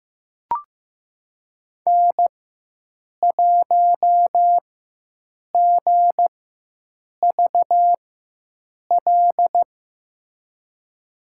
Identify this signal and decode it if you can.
Morse code, heard as a keyed tone, about 15 wpm, spaced as for 7 wpm, 700 Hz. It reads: N1GVL
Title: USA Call Signs - 5 Characters - 4x Character Spacing - 15wpm